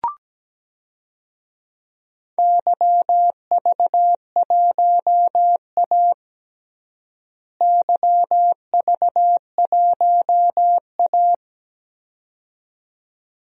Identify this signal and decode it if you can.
Morse code, heard as a keyed tone, about 17 wpm, 700 Hz. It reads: YV1A YV1A
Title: DXCC-100 Call Signs - 4 Characters - Repeated 2x - 17wpm